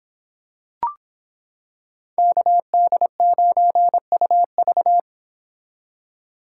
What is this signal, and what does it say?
Morse code, heard as a keyed tone, about 26 wpm, 700 Hz. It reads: KD9UV